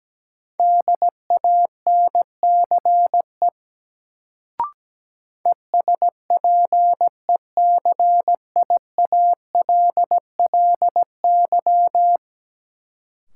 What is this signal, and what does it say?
Morse code, heard as a keyed tone, about 17 wpm, 700 Hz. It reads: DANCE ESPECIALLY
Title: Sentences from Top 700 Words - New Word Drill - 17wpm